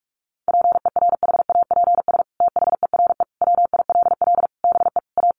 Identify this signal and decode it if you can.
Morse code, heard as a keyed tone, about 44 wpm, 700 Hz. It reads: PERHAPS THERE WILL BE A
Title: Sentences from Top 1400 Words 45wpm